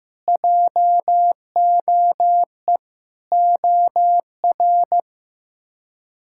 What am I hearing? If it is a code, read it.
Morse code, heard as a keyed tone, about 15 wpm, 700 Hz. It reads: JOE OR
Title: K1USN SST QSO Practice - 15wpm